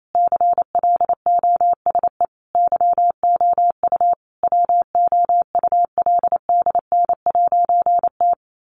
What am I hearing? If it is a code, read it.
Morse code, heard as a keyed tone, about 28 wpm, 700 Hz. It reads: CLOSE YOU WOULDN'T
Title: A Sound Of Thunder 28wpm